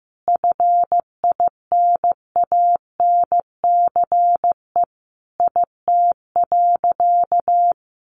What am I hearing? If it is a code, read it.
Morse code, heard as a keyed tone, about 15 wpm, 700 Hz. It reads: FINANCE IT.
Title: Sentences from Top 1500 Words 15wpm